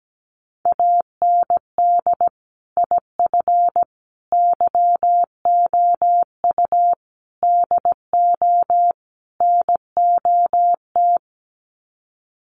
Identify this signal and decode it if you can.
Morse code, heard as a keyed tone, about 17 wpm, 700 Hz. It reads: AND IF YOU DO NOT
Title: Sets of 5 Words 17wpm